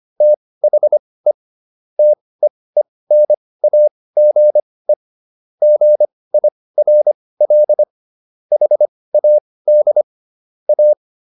Morse code, heard as a keyed tone, about 25 wpm, spaced as for 18 wpm, 600 Hz. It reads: THE TEENAGE GIRL HAD A